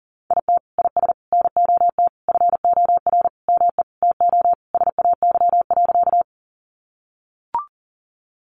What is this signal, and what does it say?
Morse code, heard as a keyed tone, about 37 wpm, 700 Hz. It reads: IT IS NOT FOR ME TO SAY.
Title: Sentences from Top 600 Words 40wpm